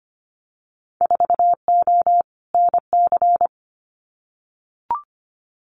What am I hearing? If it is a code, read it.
Morse code, heard as a keyed tone, about 25 wpm, 700 Hz. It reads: 4O NC
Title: Winter Field Day Exchanges Encore 25wpm